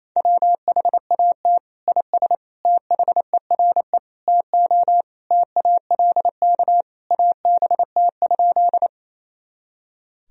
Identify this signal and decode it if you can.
Morse code, heard as a keyed tone, about 28 wpm, 700 Hz. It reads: WHAT IS THERE TO TALK ABT?
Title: Sentences from Top 400 Words and Abbreviations 28wpm